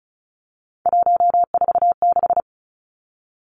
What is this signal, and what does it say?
Morse code, heard as a keyed tone, about 36 wpm, 700 Hz. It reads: 146
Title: Three Challenging Letter-Number Combinations 35wpm